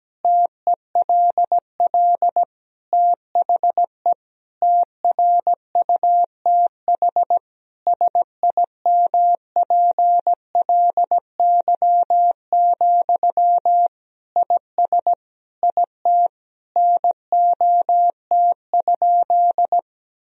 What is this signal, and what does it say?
Morse code, heard as a keyed tone, about 17 wpm, 700 Hz. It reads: TELL THE TRUTH SIMPLY, IS IT NOT?
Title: Sentences from Top 1000 Words - Review Part 4 - 17wpm